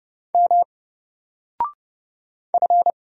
Morse code, heard as a keyed tone, about 30 wpm, 700 Hz. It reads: M F
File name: Single-Letter-Number-30wpm